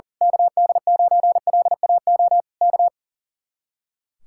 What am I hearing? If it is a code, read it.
Morse code, heard as a keyed tone, about 37 wpm, 700 Hz. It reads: KD9LAO K
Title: Speed-Racing Common QSO Phrases - 40wpm